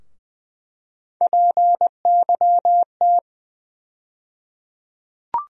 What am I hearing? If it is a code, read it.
Morse code, heard as a keyed tone, about 21 wpm, 700 Hz. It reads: PYT
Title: USA Call Sign Postfixes - 3 Letters 20wpm